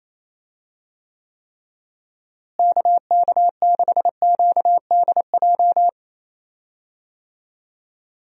Morse code, heard as a keyed tone, about 28 wpm, 700 Hz. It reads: KK6QDJ
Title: USA Call Signs - 6 Characters - Repeated 3x - 28wpm